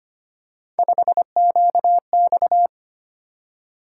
Morse code, heard as a keyed tone, about 25 wpm, 700 Hz. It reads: 5QX
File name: Three Letter-Number Combinations - 25wpm